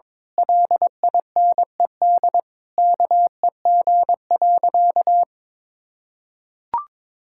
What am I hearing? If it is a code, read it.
Morse code, heard as a keyed tone, about 22 wpm, 700 Hz. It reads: LINED KEG.
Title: Pangram Challenge v2 - 22wpm